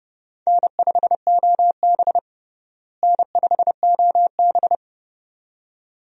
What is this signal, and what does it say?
Morse code, heard as a keyed tone, about 29 wpm, 700 Hz. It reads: N5OB N5OB
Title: USA Call Signs - 4 Characters - Repeated 2x - 30wpm